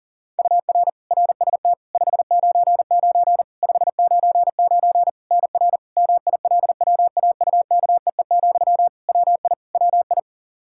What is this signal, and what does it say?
Morse code, heard as a keyed tone, about 40 wpm, 700 Hz. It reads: UR RST 599 599 NR MILWAUKEE, WI WI
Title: Practice QSO - Exchange 1 - Beginning - 40wpm